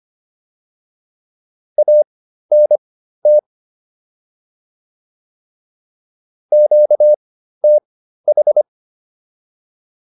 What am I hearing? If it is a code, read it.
Morse code, heard as a keyed tone, about 25 wpm, spaced as for 13 wpm, 600 Hz. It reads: ANT QTH